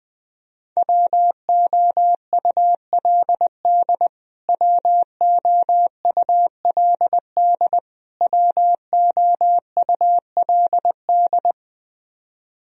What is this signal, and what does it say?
Morse code, heard as a keyed tone, about 20 wpm, 700 Hz. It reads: WOULD WOULD WOULD
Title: Top 100 Words Repeated 20wpm